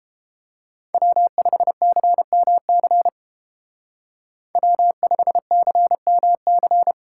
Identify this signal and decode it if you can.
Morse code, heard as a keyed tone, about 29 wpm, 700 Hz. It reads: W5CMC W5CMC